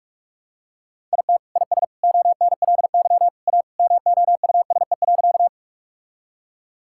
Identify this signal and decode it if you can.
Morse code, heard as a keyed tone, about 46 wpm, 700 Hz. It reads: IT IS ONLY A MOUSE.
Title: Sentences from Top 1000 Words - Review Part 5 - 45wpm